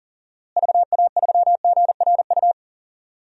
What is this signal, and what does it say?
Morse code, heard as a keyed tone, about 40 wpm, 700 Hz. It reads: VA3GRU
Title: DXCC-100 Call Signs - 6 Characters - Repeated 2x - 40wpm